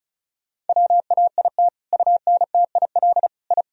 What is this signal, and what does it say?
Morse code, heard as a keyed tone, about 34 wpm, 700 Hz. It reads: WAIT UNTIL I
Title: Sentences from Top 1000 Words - Review Part 1 - 35wpm